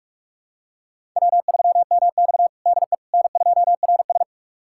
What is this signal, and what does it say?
Morse code, heard as a keyed tone, about 42 wpm, 700 Hz. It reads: W3MX DE N2RS
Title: USA Callsign DE Callsign - 4 Characters - 45wpm